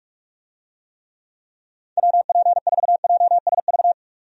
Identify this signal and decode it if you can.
Morse code, heard as a keyed tone, about 46 wpm, 700 Hz. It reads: WP4JSV
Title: USA Call Signs - 6 Characters - 45wpm